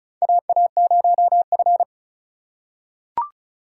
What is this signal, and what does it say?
Morse code, heard as a keyed tone, about 34 wpm, 700 Hz. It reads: AA0F
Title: USA Call Signs - 3x Character Spacing - 35wpm